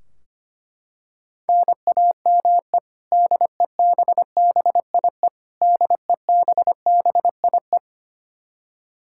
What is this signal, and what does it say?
Morse code, heard as a keyed tone, about 25 wpm, 700 Hz. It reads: NAME DEBBIE DEBBIE
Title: Practice QSO - Exchange 1 - Beginning - 25wpm